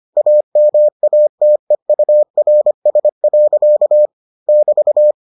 Morse code, heard as a keyed tone, about 25 wpm, 600 Hz. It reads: AMATEURS. <BT>